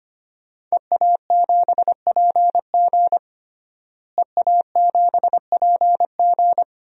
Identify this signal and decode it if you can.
Morse code, heard as a keyed tone, about 25 wpm, 700 Hz. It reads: EA7PG EA7PG
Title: DXCC-100 Call Signs - 5 Characters - Repeated 3x - 25wpm